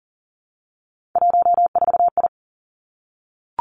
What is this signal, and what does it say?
Morse code, heard as a keyed tone, about 38 wpm, 700 Hz. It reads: 14I
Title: Three Challenging Letter-Number Combinations 40wpm